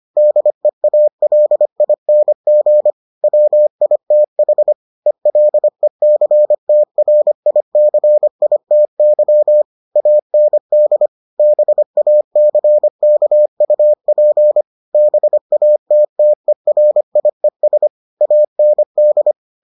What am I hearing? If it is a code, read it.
Morse code, heard as a keyed tone, about 25 wpm, 600 Hz. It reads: DEALING WITH ELECTRICITY AND BACKUP BATTERIES AND